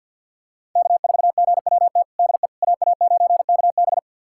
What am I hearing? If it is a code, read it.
Morse code, heard as a keyed tone, about 46 wpm, 690 Hz. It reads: K4GWT DE AA9KB
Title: USA Callsign DE Callsign 50wpm